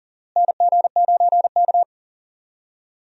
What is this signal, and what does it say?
Morse code, heard as a keyed tone, about 40 wpm, 700 Hz. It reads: NG9K
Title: USA Call Signs - 4 Characters - Repeated 2x - 40wpm